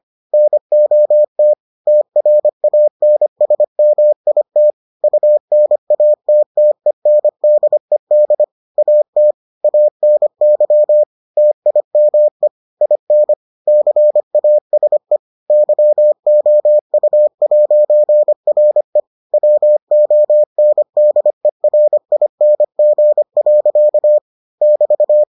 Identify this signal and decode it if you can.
Morse code, heard as a keyed tone, about 25 wpm, 600 Hz. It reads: NOT TRANSMIT UNATTENDED AT ANY TIME IN CASE YOU'RE WONDERING. <BT>